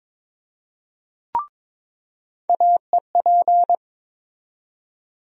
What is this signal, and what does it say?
Morse code, heard as a keyed tone, about 22 wpm, 700 Hz. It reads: AEP
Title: USA Call Sign Postfixes - 3 Letters 22wpm